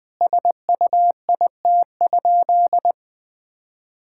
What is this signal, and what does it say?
Morse code, heard as a keyed tone, about 21 wpm, 700 Hz. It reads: SUIT?
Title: Sentences from Top 1600 Words 20wpm